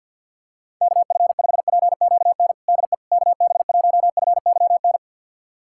Morse code, heard as a keyed tone, about 51 wpm, 700 Hz. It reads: KF5PQN DE KB1FYN
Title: USA Callsign DE Callsign - 6 Characters - 50wpm